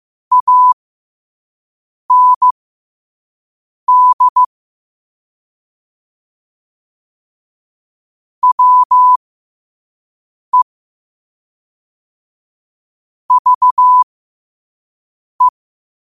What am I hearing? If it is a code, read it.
Morse code, heard as a keyed tone, about 15 wpm, spaced as for 5 wpm, 1000 Hz. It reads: AND WE VE
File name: qotdfiles / 5wpm